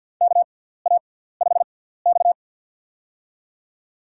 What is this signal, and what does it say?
Morse code, heard as a keyed tone, about 48 wpm, spaced as for 17 wpm, 700 Hz. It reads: KA5X
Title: USA Call Signs - 4 Characters - 6x Character Spacing - 50wpm